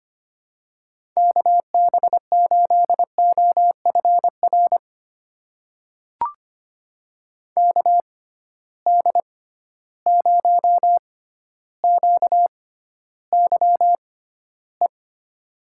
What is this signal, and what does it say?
Morse code, heard as a keyed tone, about 25 wpm, spaced as for 13 wpm, 700 Hz. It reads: KB8OFR KD0QYE
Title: USA Call Signs - 6 Characters - 6x Character Spacing - 25wpm